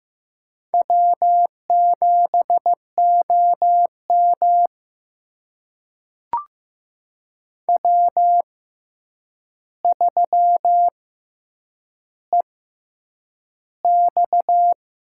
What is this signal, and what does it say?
Morse code, heard as a keyed tone, about 15 wpm, spaced as for 8 wpm, 700 Hz. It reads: W7OM W3EX
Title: USA Call Signs - 4 Characters - 6x Character Spacing - 15wpm